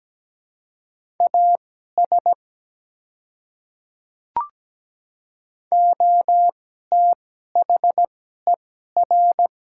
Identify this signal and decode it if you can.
Morse code, heard as a keyed tone, about 17 wpm, spaced as for 12 wpm, 700 Hz. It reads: AS OTHER
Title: 2x Character Spacing - Top 100 Words - 17wpm